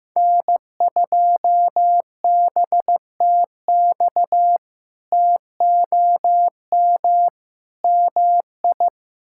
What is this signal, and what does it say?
Morse code, heard as a keyed tone, about 15 wpm, 700 Hz. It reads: N2BTX TOM MI
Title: K1USN SST QSO Practice - 15wpm